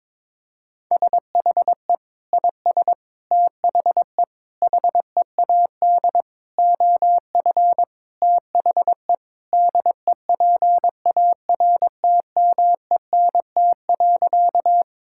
Morse code, heard as a keyed tone, about 22 wpm, 700 Hz. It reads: SHE IS THE HEAD OF THE DEPARTMENT.
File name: practice / 22wpm / Sentences with 4-Word Phrases - 22wpm